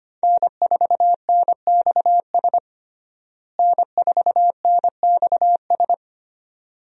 Morse code, heard as a keyed tone, about 25 wpm, 700 Hz. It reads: N4NXS N4NXS
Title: USA Call Signs - 5 Characters - Repeated 2x - 25wpm